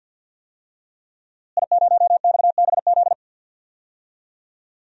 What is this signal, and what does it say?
Morse code, heard as a keyed tone, about 52 wpm, 700 Hz. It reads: I0XBZ
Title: DXCC-100 Call Signs - 5 Characters - Repeated 2x - 50wpm